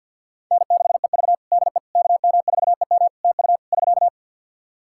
Morse code, heard as a keyed tone, about 48 wpm, 700 Hz. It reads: N6EV DE KM4EM TU <SK>
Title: Practice QSO - Exchange 4 - Ending - 50wpm